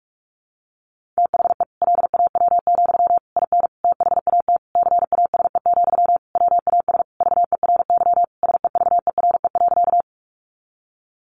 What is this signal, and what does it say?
Morse code, heard as a keyed tone, about 44 wpm, 700 Hz. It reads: THE LAW, IN THAT CASE, WAS VERY SEVERE.
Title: Sentences from Top 1400 Words 45wpm